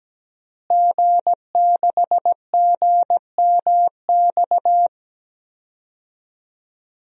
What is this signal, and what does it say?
Morse code, heard as a keyed tone, about 17 wpm, 700 Hz. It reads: G6GMX
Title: DXCC-100 Call Signs - 5 Characters - Repeated 3x - 17wpm